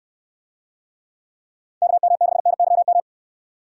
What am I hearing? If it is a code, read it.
Morse code, heard as a keyed tone, about 69 wpm, 700 Hz. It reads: KM6NYM